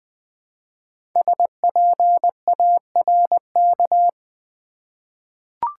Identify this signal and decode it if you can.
Morse code, heard as a keyed tone, about 20 wpm, 700 Hz. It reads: SPARK
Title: 5 Letter Words - First 3 Letters Given - 20wpm